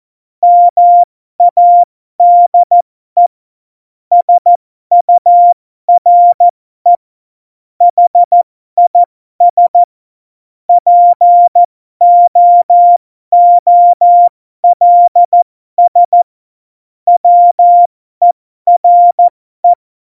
Morse code, heard as a keyed tone, about 14 wpm, spaced as for 12 wpm, 700 Hz. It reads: MADE SURE HIS POOLS WERE
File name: Morsekurs / Witze